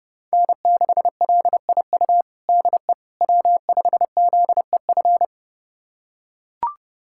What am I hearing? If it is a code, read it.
Morse code, heard as a keyed tone, about 30 wpm, 700 Hz. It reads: N6LIU DE W5ZEF
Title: USA Callsign DE Callsign - 5 Characters - 30wpm